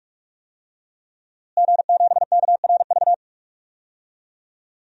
Morse code, heard as a keyed tone, about 41 wpm, 700 Hz. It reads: G7KRV
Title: DXCC-100 Call Signs - 5 Characters - Repeated 2x - 45wpm